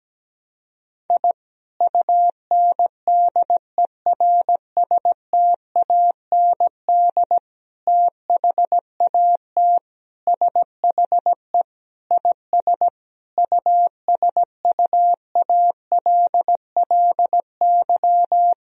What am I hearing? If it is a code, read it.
Morse code, heard as a keyed tone, about 17 wpm, 700 Hz. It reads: I UNDERSTAND THAT SHE IS USUALLY
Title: Sentences from Top 1000 Words - Review Part 3 - 17wpm